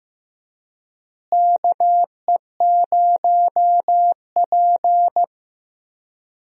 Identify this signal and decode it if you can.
Morse code, heard as a keyed tone, about 15 wpm, 700 Hz. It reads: KE0P
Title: USA Call Signs - 4 Characters - Repeated 2x - 15wpm